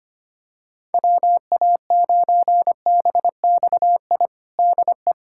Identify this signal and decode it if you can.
Morse code, heard as a keyed tone, about 25 wpm, 700 Hz. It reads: WA9BXI DE
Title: Practice QSO - Exchange 4 - Ending - 25wpm